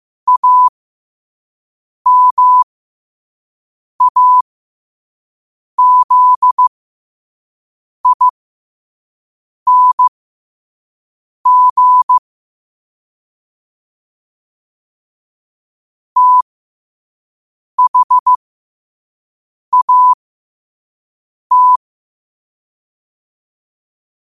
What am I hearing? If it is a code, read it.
Morse code, heard as a keyed tone, about 15 wpm, spaced as for 5 wpm, 1000 Hz. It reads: AMAZING THAT
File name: qotdfiles / 5wpm